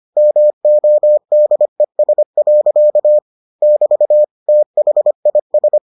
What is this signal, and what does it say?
Morse code, heard as a keyed tone, about 25 wpm, 600 Hz. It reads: MODES. <BT> THIS